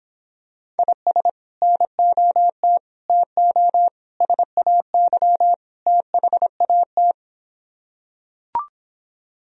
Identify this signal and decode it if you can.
Morse code, heard as a keyed tone, about 26 wpm, 700 Hz. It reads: IS NOT TO SAY THAT